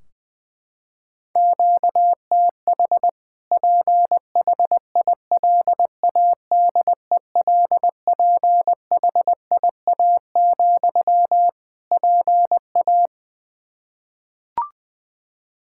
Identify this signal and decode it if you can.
Morse code, heard as a keyed tone, about 20 wpm, 700 Hz. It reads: QTH PHILADELPHIA, PA
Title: Practice QSO - Exchange 1 - Beginning - 20wpm